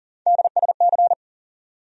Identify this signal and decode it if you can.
Morse code, heard as a keyed tone, about 40 wpm, 700 Hz. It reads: DSC
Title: USA Call Sign Postfixes - 3 Letters 40wpm